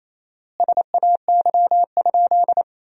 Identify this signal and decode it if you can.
Morse code, heard as a keyed tone, about 28 wpm, 700 Hz. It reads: SAY?
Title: 4x Word-Spacing - Sentences from Top 200 Words - 28wpm